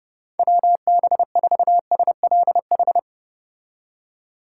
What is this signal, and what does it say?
Morse code, heard as a keyed tone, about 30 wpm, 700 Hz. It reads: WB4SLH